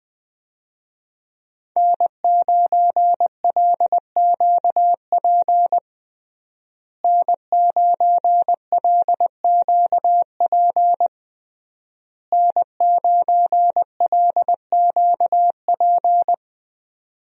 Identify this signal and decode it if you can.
Morse code, heard as a keyed tone, about 20 wpm, 700 Hz. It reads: N9LQP N9LQP N9LQP
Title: USA Call Signs - 5 Characters - Repeated 3x - 20wpm